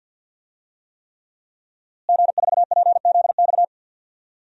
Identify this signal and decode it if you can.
Morse code, heard as a keyed tone, about 46 wpm, 700 Hz. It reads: G4PZX
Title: DXCC-100 Call Signs - 5 Characters - Repeated 3x - 50wpm